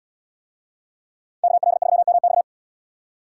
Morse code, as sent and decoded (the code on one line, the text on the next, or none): -.- .-.. ...-- -- -..-
KL3MX